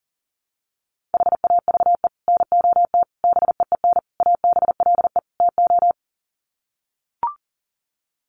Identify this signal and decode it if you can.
Morse code, heard as a keyed tone, about 37 wpm, 700 Hz. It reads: HAVE NOT BEEN ABLE TO
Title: Sets of 5 Words Encore 40wpm